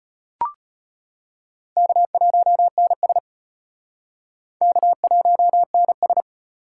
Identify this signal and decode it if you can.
Morse code, heard as a keyed tone, about 32 wpm, 700 Hz. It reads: K1NS K1NS